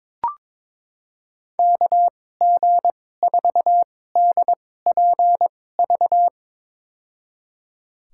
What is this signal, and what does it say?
Morse code, heard as a keyed tone, about 22 wpm, spaced as for 16 wpm, 700 Hz. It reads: KG4DPV
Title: USA Call Signs - 6 Characters - 2x Character Spacing - 22wpm